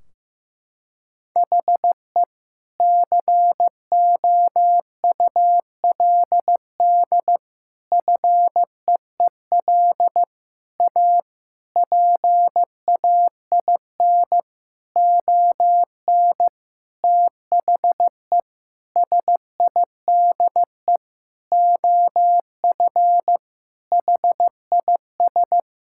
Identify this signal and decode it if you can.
Morse code, heard as a keyed tone, about 15 wpm, 700 Hz. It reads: HE COULD FEEL A PAIN ON THE SIDE OF HIS
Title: Sentences from Top 1600 Words 15wpm